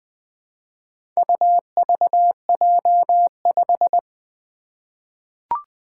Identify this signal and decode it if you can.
Morse code, heard as a keyed tone, about 20 wpm, 700 Hz. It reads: UVJ5
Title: Four Challenging Letter-Number Combinations 20wpm